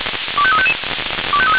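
Five-tone selective call (ZVEI), heard as quick runs of five tones, tones of about 70 ms. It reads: 25355
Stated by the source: Five-tone ZVEI selective call received on an SDR.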